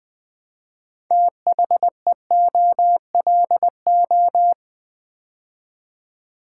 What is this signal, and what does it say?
Morse code, heard as a keyed tone, about 20 wpm, 700 Hz. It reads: THEOLO
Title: Build A Word - 2 Letters At A Time - 8-Letter Words - 20wpm